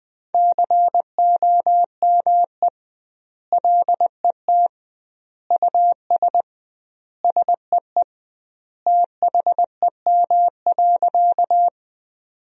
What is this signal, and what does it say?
Morse code, heard as a keyed tone, about 21 wpm, 700 Hz. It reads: COME LET US SEE THEM.
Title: 2x Word-Spacing - Sentences from Top 300 Words - 20wpm